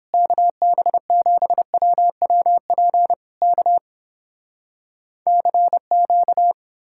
Morse code, heard as a keyed tone, about 32 wpm, 700 Hz. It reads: KB7WWP K CQ